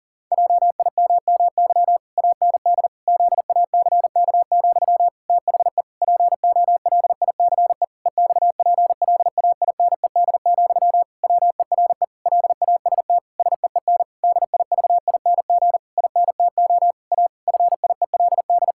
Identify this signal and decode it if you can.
Morse code, heard as a keyed tone, about 37 wpm, 700 Hz. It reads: JIMMY AND ZACK, THE POLICE EXPLAINED, WERE LAST SEEN DIVING INTO A FIELD
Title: Pangram Challenge v2 - 40wpm